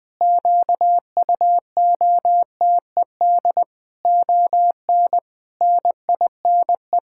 Morse code, heard as a keyed tone, about 19 wpm, 700 Hz. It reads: QUOTED ON NINE
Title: Pangram Challenge v2 - 20wpm